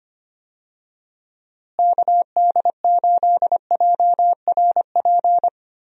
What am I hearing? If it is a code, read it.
Morse code, heard as a keyed tone, about 25 wpm, 700 Hz. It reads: KD8JRP
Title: USA Call Signs - 6 Characters - Repeated 3x - 25wpm